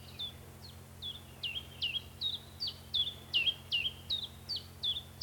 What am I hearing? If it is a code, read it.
There is no code.